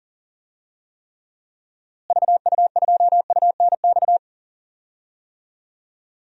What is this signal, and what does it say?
Morse code, heard as a keyed tone, about 37 wpm, 700 Hz. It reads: VU2UNX
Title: DXCC-100 Call Signs - 6 Characters - Repeated 3x - 40wpm